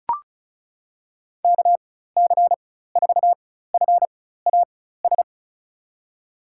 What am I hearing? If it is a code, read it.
Morse code, heard as a keyed tone, about 34 wpm, spaced as for 16 wpm, 700 Hz. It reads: KC4FAS
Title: USA Call Signs - 6 Characters - 4x Character Spacing - 35wpm